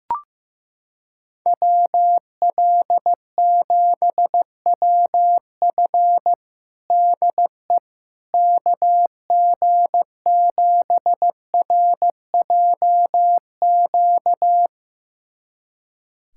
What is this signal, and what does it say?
Morse code, heard as a keyed tone, about 15 wpm, 700 Hz. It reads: WL7WF DE KG7RJQ
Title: USA Callsign DE Callsign 15wpm